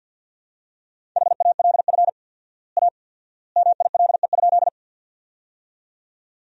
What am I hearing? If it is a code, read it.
Morse code, heard as a keyed tone, about 50 wpm, 700 Hz. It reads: HALF A MILE?
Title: 4x Word-Spacing - Sentences from Top 400 Words - 50wpm